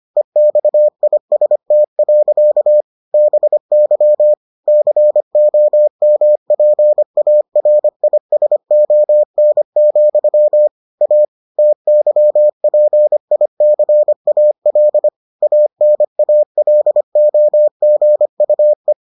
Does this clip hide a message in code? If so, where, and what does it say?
Yes: Morse code from 0.2 to 18.9 s, heard as a keyed tone, about 25 wpm, 600 Hz. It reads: EXIST. BY COMPARISON, A TYPICAL ANALOGUE